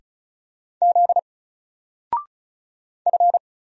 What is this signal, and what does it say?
Morse code, heard as a keyed tone, about 36 wpm, 700 Hz. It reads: Z F
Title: Single Letter-Number 35wpm